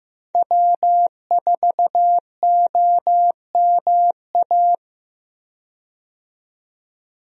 Morse code, heard as a keyed tone, about 15 wpm, 700 Hz. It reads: W4OMA
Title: USA Call Signs - 5 Characters - 15wpm